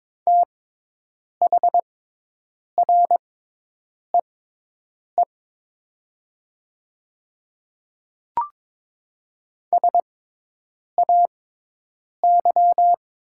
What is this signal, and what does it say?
Morse code, heard as a keyed tone, about 22 wpm, spaced as for 8 wpm, 700 Hz. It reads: THREE SAY
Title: 6x Character Spacing - Top 300 Words - 22wpm